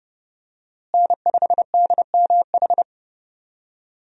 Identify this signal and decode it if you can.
Morse code, heard as a keyed tone, about 30 wpm, 700 Hz. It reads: N5DMH
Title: USA Call Signs - 5 Characters - Repeated 2x - 30wpm